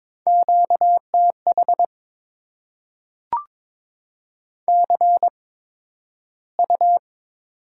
Morse code, heard as a keyed tone, about 22 wpm, spaced as for 12 wpm, 700 Hz. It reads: QTH CU
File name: Top 20 QSO Elements - 8x Character Spacing - 22wpm